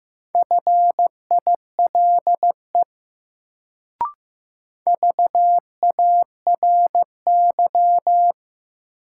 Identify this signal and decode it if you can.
Morse code, heard as a keyed tone, about 15 wpm, 700 Hz. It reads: FILE VARY
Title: Sentences from Top 1200 Words - New Word Drill Alt - 15wpm